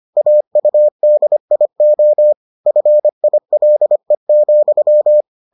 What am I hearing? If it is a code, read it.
Morse code, heard as a keyed tone, about 25 wpm, 600 Hz. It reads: AUDIO FILE,